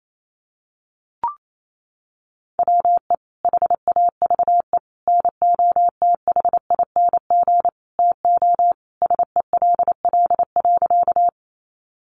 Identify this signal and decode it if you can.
Morse code, heard as a keyed tone, about 28 wpm, 700 Hz. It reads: WE HAVE NOTHING TO SELL.
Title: Sentences from Top 1100 Words 28wpm